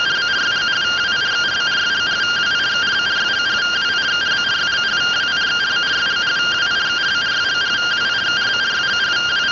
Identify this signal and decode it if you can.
Two-tone teleprinter signal, about 170 Hz shift.
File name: findings / CT 205 ax mips, iDose 3 0006.gen-45 rtty rx